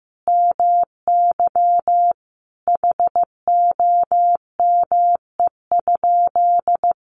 Morse code, heard as a keyed tone, about 15 wpm, 700 Hz. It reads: MY HOME?
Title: Sentences from Top 1200 Words 15wpm